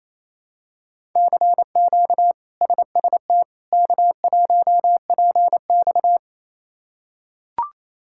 Morse code, heard as a keyed tone, about 28 wpm, 700 Hz. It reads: CQ SST K1PX
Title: K1USN SST QSO Practice - 28wpm